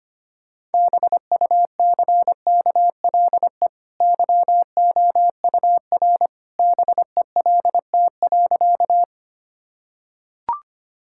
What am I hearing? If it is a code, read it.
Morse code, heard as a keyed tone, about 25 wpm, 700 Hz. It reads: BUCKLE YOUR BELT.
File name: Easy Sentences with Hints - 25wpm